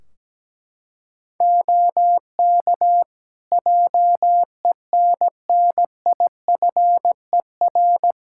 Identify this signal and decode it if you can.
Morse code, heard as a keyed tone, about 17 wpm, 700 Hz. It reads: OK JENNIFER
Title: Practice QSO - Exchange 3 - More Info - 17wpm